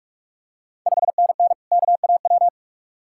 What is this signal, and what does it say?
Morse code, heard as a keyed tone, about 44 wpm, 690 Hz. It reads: 5NN KAW
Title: Ontario QSO Party Exchange 45wpm